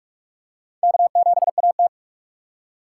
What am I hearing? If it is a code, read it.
Morse code, heard as a keyed tone, about 44 wpm, 700 Hz. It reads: K7AT
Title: USA Call Signs - 4 Characters - 6x Character Spacing - 45wpm